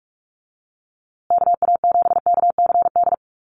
Morse code, heard as a keyed tone, about 44 wpm, 700 Hz. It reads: KA7KCD
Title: USA Call Signs Repeated 3x 45wpm